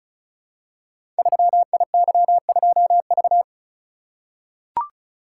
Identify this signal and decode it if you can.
Morse code, heard as a keyed tone, about 36 wpm, 700 Hz. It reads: 3IY2V
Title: Code Groups - 5 Letter:Numbers - 35wpm